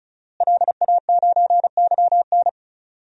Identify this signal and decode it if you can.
Morse code, heard as a keyed tone, about 34 wpm, 700 Hz. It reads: LA9YN
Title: DXCC-100 Call Signs - 5 Characters - Repeated 2x - 35wpm